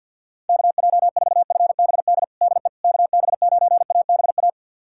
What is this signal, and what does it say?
Morse code, heard as a keyed tone, about 52 wpm, 700 Hz. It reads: KJ4FBD DE KB9ABA
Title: USA Callsign DE Callsign - 6 Characters - 50wpm